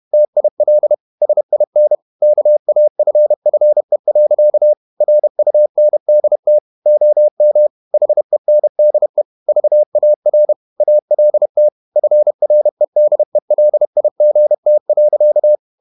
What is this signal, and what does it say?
Morse code, heard as a keyed tone, about 31 wpm, 600 Hz. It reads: TIL SIN KAFFE. RUNDT OM HENDE VAR ALT FREDELIGT.
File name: Ferie QRQ 31wpm